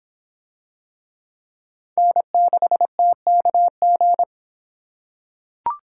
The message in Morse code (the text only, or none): N6TKG